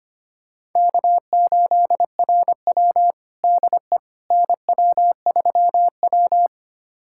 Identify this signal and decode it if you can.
Morse code, heard as a keyed tone, about 25 wpm, 700 Hz. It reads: K8RW DE NW3W
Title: USA Callsign DE Callsign - 4 Characters - 25wpm